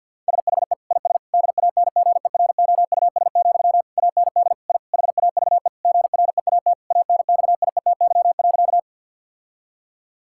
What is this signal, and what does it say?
Morse code, heard as a keyed tone, about 51 wpm, 700 Hz. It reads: SHE IS DANGEROUS, AND I HAVE GREAT ANXIETY.
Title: Sentences from Top 1600 Words 50wpm